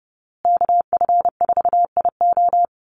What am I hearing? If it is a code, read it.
Morse code, heard as a keyed tone, about 30 wpm, 700 Hz. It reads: KF4IO
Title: USA Call Signs Repeated 2x 30wpm